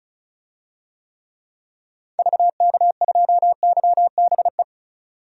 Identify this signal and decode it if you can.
Morse code, heard as a keyed tone, about 36 wpm, 700 Hz. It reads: VK2YBE
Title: DXCC-100 Call Signs - 6 Characters - 35wpm